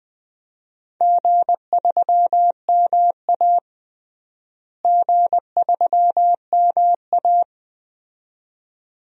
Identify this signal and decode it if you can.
Morse code, heard as a keyed tone, about 20 wpm, 700 Hz. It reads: G3MA G3MA
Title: DXCC-100 Call Signs - 4 Characters - Repeated 2x - 20wpm